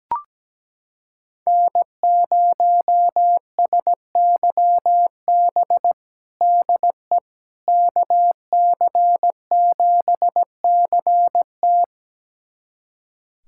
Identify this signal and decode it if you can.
Morse code, heard as a keyed tone, about 17 wpm, 700 Hz. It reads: N0SYB DE KC7CT
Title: USA Callsign DE Callsign - 5 Characters - 17wpm